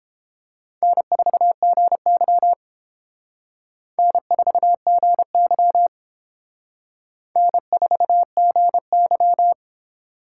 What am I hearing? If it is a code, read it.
Morse code, heard as a keyed tone, about 29 wpm, 700 Hz. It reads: N4GY N4GY N4GY